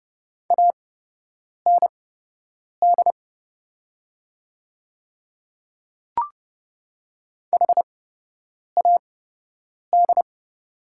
Morse code, heard as a keyed tone, about 32 wpm, spaced as for 8 wpm, 700 Hz. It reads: AND HAD
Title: 8x Character Spacing - Top 200 Words - 30wpm